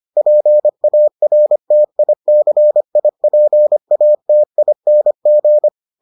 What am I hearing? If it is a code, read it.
Morse code, heard as a keyed tone, about 25 wpm, 600 Hz. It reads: PARTICIPATING